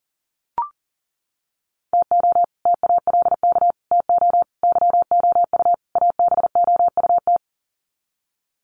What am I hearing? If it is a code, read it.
Morse code, heard as a keyed tone, about 38 wpm, 700 Hz. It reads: TO TALK TO YOU ABOUT
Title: Sets of 5 Words Encore 40wpm